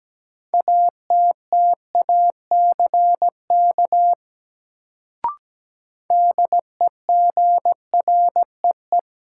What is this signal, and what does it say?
Morse code, heard as a keyed tone, about 17 wpm, 700 Hz. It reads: ATTACK DEGREE
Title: Sentences from Top 1500 Words - New Word Drill Alt - 17wpm